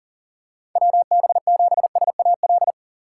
Morse code, heard as a keyed tone, about 38 wpm, 700 Hz. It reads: WB7SAL
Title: USA Call Signs - 6 Characters - 3x Character Spacing - 40wpm